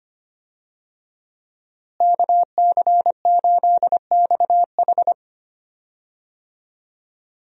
Morse code, heard as a keyed tone, about 25 wpm, 700 Hz. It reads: KC8XH
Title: USA Call Signs - 5 Characters - Repeated 3x - 25wpm